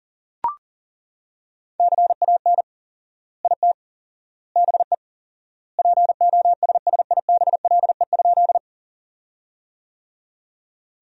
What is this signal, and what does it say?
Morse code, heard as a keyed tone, about 40 wpm, 700 Hz. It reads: CAN IT BE POSSIBLE?
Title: 4x Word-Spacing - Sentences from Top 500 Words - 40wpm